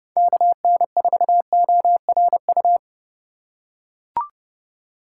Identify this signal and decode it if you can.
Morse code, heard as a keyed tone, about 30 wpm, 700 Hz. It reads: KN4ORU
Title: USA Call Signs - 6 Characters - 3x Character Spacing - 30wpm